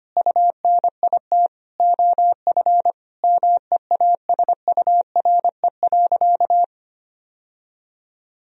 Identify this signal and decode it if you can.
Morse code, heard as a keyed tone, about 25 wpm, 700 Hz. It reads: UNIT OF MEASURE.
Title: Sentences from Top 1000 Words - Review Part 2 - 25wpm